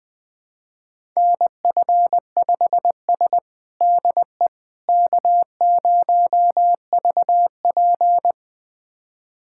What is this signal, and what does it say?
Morse code, heard as a keyed tone, about 21 wpm, 700 Hz. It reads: NF5S DE K0VP
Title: USA Callsign DE Callsign - 4 Characters - 20wpm